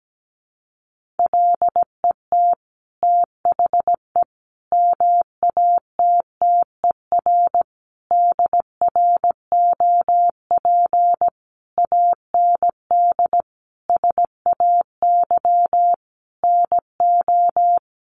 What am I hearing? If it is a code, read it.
Morse code, heard as a keyed tone, about 17 wpm, 700 Hz. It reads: LET THE MATTER DROP AND SAY NO
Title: Sentences from Top 1400 Words 17wpm